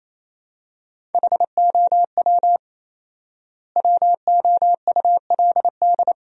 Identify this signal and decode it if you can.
Morse code, heard as a keyed tone, about 27 wpm, 700 Hz. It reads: HOW WOULD
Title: 4x Word-Spacing - Sentences from Top 100 Words - 28wpm